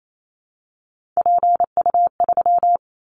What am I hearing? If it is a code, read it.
Morse code, heard as a keyed tone, about 28 wpm, 700 Hz. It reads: PU3
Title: Callsign Prefixes for Top 100 DXCC - 28wpm